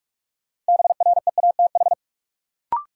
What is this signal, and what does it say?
Morse code, heard as a keyed tone, about 46 wpm, 700 Hz. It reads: BREATH
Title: Sets of 5 Words Encore - New Word Drill - 45wpm